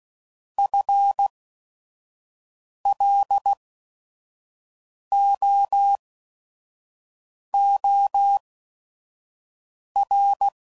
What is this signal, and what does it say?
Morse code, heard as a keyed tone, about 16 wpm, spaced as for 5 wpm, 800 Hz. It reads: FLOOR
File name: Prefixes / By First Letter / Fs